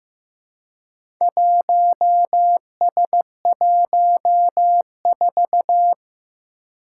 Morse code, heard as a keyed tone, about 15 wpm, 700 Hz. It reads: 1S14
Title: Four Challenging Letter-Number Combinations 15wpm